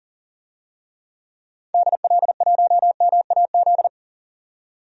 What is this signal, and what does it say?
Morse code, heard as a keyed tone, about 40 wpm, 700 Hz. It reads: DL1MAZ
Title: DXCC-100 Call Signs - 6 Characters - 40wpm